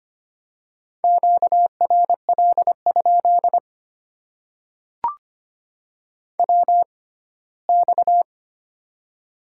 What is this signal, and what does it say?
Morse code, heard as a keyed tone, about 25 wpm, spaced as for 17 wpm, 700 Hz. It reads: QRL? WX
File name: Top 90 QSO Elements - 6x Character Spacing - 25wpm